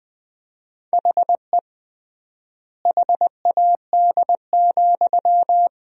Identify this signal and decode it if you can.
Morse code, heard as a keyed tone, about 21 wpm, 700 Hz. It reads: HE HAD,
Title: 3x Word-Spacing - Sentences from Top 500 Words - 20wpm